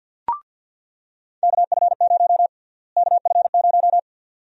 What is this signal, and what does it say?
Morse code, heard as a keyed tone, about 50 wpm, 700 Hz. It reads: KF0 KF0
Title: USA Call Sign Prefixes - Repeated 2x - 50wpm